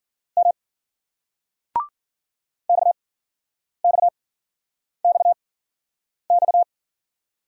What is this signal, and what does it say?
Morse code, heard as a keyed tone, about 56 wpm, spaced as for 9 wpm, 700 Hz. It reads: M XXXX